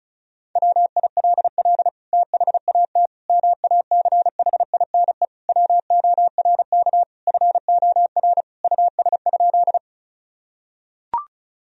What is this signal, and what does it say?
Morse code, heard as a keyed tone, about 34 wpm, 700 Hz. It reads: WILL THAT MACHINE WORK FOR US?
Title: Sentences from Top 1000 Words - Review Part 2 - 35wpm